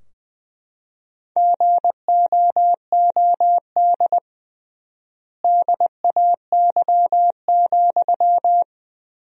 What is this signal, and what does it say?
Morse code, heard as a keyed tone, about 20 wpm, 700 Hz. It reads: GOOD DAY,
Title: 3x Word-Spacing - Sentences from Top 200 Words - 20wpm